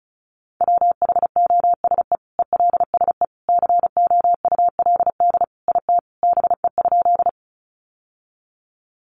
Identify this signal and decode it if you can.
Morse code, heard as a keyed tone, about 34 wpm, 700 Hz. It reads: WHOSE ELSE COULD IT BE?
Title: Sentences from Top 1200 Words 35wpm